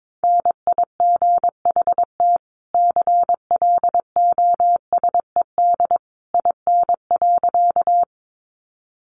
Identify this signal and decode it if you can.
Morse code, heard as a keyed tone, about 22 wpm, 700 Hz. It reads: NIGHT CLOSED IN.
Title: Sentences from Top 1400 Words 22wpm